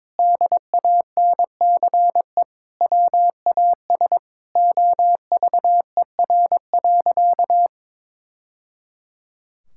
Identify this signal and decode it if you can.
Morse code, heard as a keyed tone, about 23 wpm, 700 Hz. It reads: DANCE WAS OVER.
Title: Sentences from Top 1000 Words - Review Part 6 - 22wpm